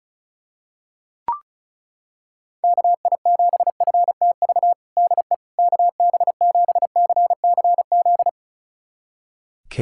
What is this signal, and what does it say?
Morse code, heard as a keyed tone, about 36 wpm, 700 Hz. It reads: KI7FTV DE KB7CCZ
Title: USA Callsign DE Callsign - 6 Characters - 35wpm